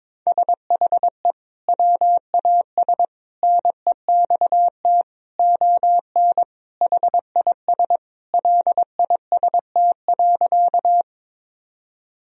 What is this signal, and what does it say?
Morse code, heard as a keyed tone, about 22 wpm, 700 Hz. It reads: SHE WAS NEXT ON HIS LIST.
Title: Sentences from Top 1000 Words - Review Part 2 - 22wpm